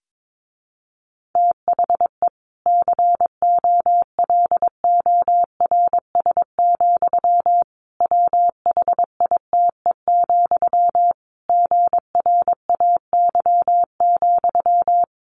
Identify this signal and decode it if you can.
Morse code, heard as a keyed tone, about 22 wpm, 700 Hz. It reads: THE COLORS, WHITE, GRAY,
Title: A Sound Of Thunder 22wpm